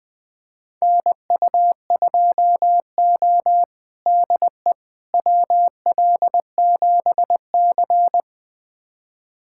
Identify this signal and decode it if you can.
Morse code, heard as a keyed tone, about 21 wpm, 700 Hz. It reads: NU2O DE WL7C
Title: USA Callsign DE Callsign - 4 Characters - 20wpm